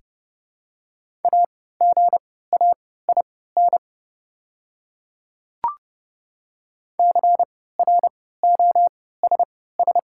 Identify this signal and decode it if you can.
Morse code, heard as a keyed tone, about 30 wpm, spaced as for 17 wpm, 700 Hz. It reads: AGAIN CROSS
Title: 3x Character Spacing - Top 300 Words - 30wpm